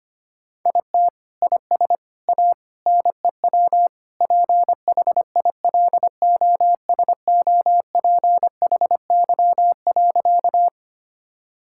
Morse code, heard as a keyed tone, about 25 wpm, 700 Hz. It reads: IT IS A NEW PHILOSOPHY.
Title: Sentences from Top 1000 Words - Review Part 6 - 25wpm